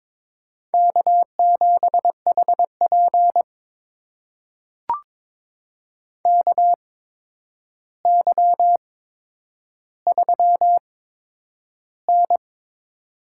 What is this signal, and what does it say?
Morse code, heard as a keyed tone, about 22 wpm, spaced as for 9 wpm, 700 Hz. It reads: K7HP KY3N
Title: USA Call Signs - 4 Characters - 8x Character Spacing - 22wpm